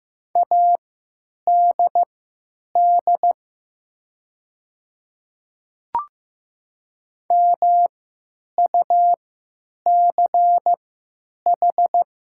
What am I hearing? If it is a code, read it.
Morse code, heard as a keyed tone, about 15 wpm, spaced as for 9 wpm, 700 Hz. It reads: ADD MUCH
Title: 3x Character Spacing - Top 300 Words - 15wpm